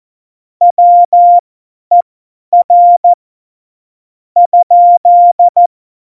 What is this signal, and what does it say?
Morse code, heard as a keyed tone, about 14 wpm, spaced as for 10 wpm, 700 Hz. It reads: WER ?
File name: Morsekurs / Witze / Witz 23. Oktober 2018, 50 F70